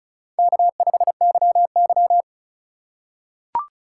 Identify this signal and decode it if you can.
Morse code, heard as a keyed tone, about 36 wpm, 700 Hz. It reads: K5YY
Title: USA Call Signs - 4 Characters - 2x Character Spacing - 35wpm